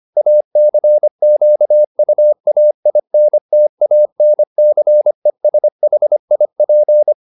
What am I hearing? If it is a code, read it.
Morse code, heard as a keyed tone, about 25 wpm, 600 Hz. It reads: ACQUAINTANCESHIP